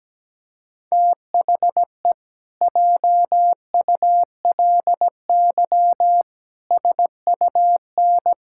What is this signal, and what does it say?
Morse code, heard as a keyed tone, about 17 wpm, 700 Hz. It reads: THE JULY SUN
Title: Pangram Challenge v2 - 17wpm